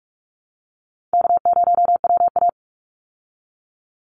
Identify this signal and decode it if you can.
Morse code, heard as a keyed tone, about 44 wpm, 700 Hz. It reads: K0WA
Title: Call Signs Repeated 3x - 45wpm